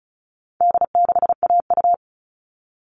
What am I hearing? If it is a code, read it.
Morse code, heard as a keyed tone, about 34 wpm, 700 Hz. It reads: D6AU
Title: All Cut Numbers - 4-Digit 35wpm